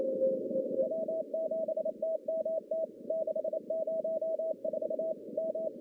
Morse code, heard as a keyed tone, about 29 wpm, 640 Hz, with noise. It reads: W7TMT 604 M
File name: contests / CW-SS-k5tr- -1200